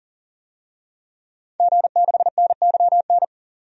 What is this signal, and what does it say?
Morse code, heard as a keyed tone, about 38 wpm, 700 Hz. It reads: G6NYN
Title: DXCC-100 Call Signs - 5 Characters - 40wpm